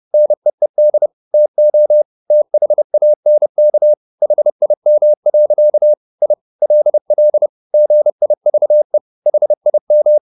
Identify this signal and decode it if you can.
Morse code, heard as a keyed tone, about 30 wpm, 600 Hz. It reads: NEED TO THANK HIM. I LL GIVE HIM
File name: the old man and the sea / the old man and the sea 1 30wpm